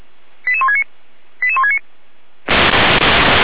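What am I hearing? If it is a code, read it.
Five-tone selective call (ZVEI), heard as quick runs of five tones, tones of about 70 ms. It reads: 88169 88169
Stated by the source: ZVEI five-tone call received by SDR, from the feuerwehr-orlamuende.org fire brigade audio server.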